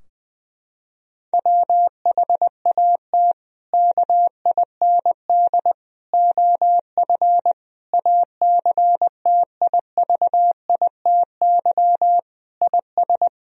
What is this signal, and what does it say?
Morse code, heard as a keyed tone, about 21 wpm, 700 Hz. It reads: WHAT KIND OF ACTIVITY IS
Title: Sentences from Top 1000 Words - Review Part 4 - 20wpm